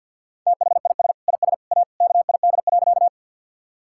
Morse code, heard as a keyed tone, about 48 wpm, 700 Hz. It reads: THIS IS A KID.
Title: Sentences from Top 1600 Words 50wpm